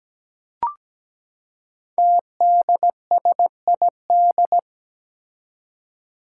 Morse code, heard as a keyed tone, about 17 wpm, 700 Hz. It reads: TDSID
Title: Five Challenging Letter-Number Combinations 17wpm